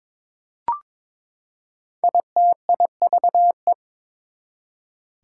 Morse code, heard as a keyed tone, about 22 wpm, 700 Hz. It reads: ITIVE
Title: Prefixes and Suffixes 22wpm